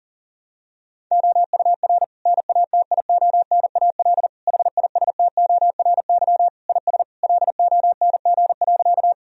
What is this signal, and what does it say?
Morse code, heard as a keyed tone, about 40 wpm, 700 Hz. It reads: OUR NATIONAL HISTORY IS LONG.
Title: Sentences from Top 1000 Words - Review Part 4 - 40wpm